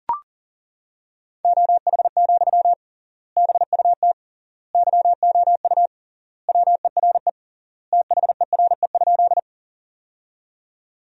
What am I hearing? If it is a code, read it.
Morse code, heard as a keyed tone, about 40 wpm, 700 Hz. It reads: OH, BUT YOU WERE THERE?
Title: 3x Word-Spacing - Sentences from Top 500 Words - 40wpm